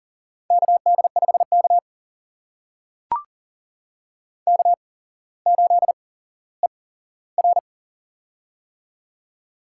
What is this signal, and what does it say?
Morse code, heard as a keyed tone, about 39 wpm, spaced as for 17 wpm, 700 Hz. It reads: KD5K K8ER
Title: USA Call Signs - 4 Characters - 8x Character Spacing - 40wpm